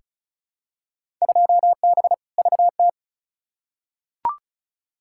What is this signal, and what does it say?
Morse code, heard as a keyed tone, about 34 wpm, 700 Hz. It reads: 2B VT
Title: Field Day Exchanges Encore - 35wpm